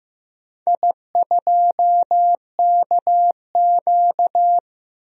Morse code, heard as a keyed tone, about 15 wpm, 700 Hz. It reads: I2KQ
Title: DXCC-100 Call Signs - 4 Characters - 15wpm